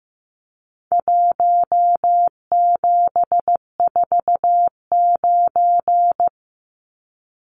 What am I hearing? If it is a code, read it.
Morse code, heard as a keyed tone, about 15 wpm, 700 Hz. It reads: 1749
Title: Numbers - Contest Style - 4-Digit 15wpm